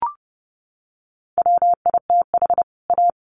Morse code, heard as a keyed tone, about 30 wpm, 700 Hz. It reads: WITH A
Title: Sets of 2 Words 30wpm